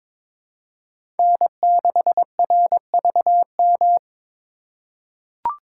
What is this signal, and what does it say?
Morse code, heard as a keyed tone, about 21 wpm, 700 Hz. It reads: N6RVM
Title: USA Call Signs - 2x Character Spacing - 22wpm